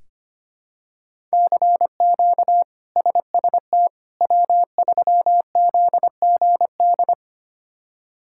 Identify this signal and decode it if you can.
Morse code, heard as a keyed tone, about 25 wpm, 700 Hz. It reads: CQ SST W3ZGD
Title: K1USN SST QSO Practice - 25wpm